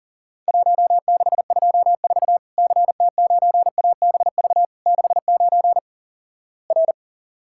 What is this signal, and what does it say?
Morse code, heard as a keyed tone, about 43 wpm, 700 Hz. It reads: 1624 CT9ABV 69 R
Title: Worked All Europe DX Contest - QTC Exchange - 40wpm